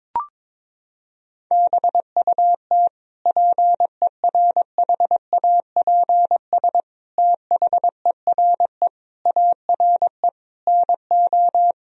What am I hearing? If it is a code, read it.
Morse code, heard as a keyed tone, about 22 wpm, 700 Hz. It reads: BUT PERHAPS THERE ARE NO
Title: Sentences from Top 1000 Words - Review Part 3 - 22wpm